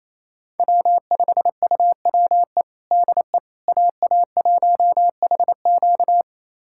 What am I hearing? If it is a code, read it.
Morse code, heard as a keyed tone, about 28 wpm, 700 Hz. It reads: W5UWE DE AA1HQ
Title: USA Callsign DE Callsign - 5 Characters - 28wpm